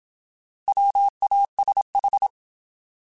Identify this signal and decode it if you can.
Morse code, heard as a keyed tone, about 26 wpm, 800 Hz. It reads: WASH
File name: Alphabetical / Alphabetical mp3s / Ws @25